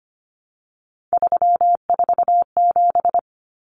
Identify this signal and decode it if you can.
Morse code, heard as a keyed tone, about 25 wpm, 700 Hz. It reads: 347
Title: Numbers - 3-Digit 25wpm